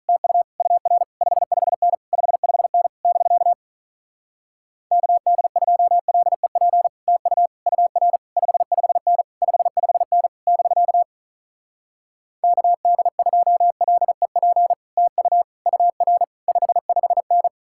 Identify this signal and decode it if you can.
Morse code, heard as a keyed tone, about 47 wpm, 700 Hz. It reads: TU UR 55N 55N <BK> KD2LEP TU UR 55N 55N <BK> KD2LEP TU UR 55N